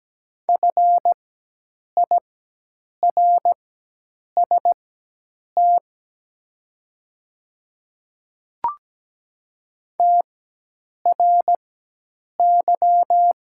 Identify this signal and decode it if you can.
Morse code, heard as a keyed tone, about 17 wpm, spaced as for 8 wpm, 700 Hz. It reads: FIRST TRY
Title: 4x Character Spacing - Top 200 Words - 17wpm